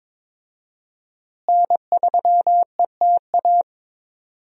Morse code, heard as a keyed tone, about 22 wpm, 700 Hz. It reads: N3ETA
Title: USA Call Signs - 5 Characters - Repeated 2x - 22wpm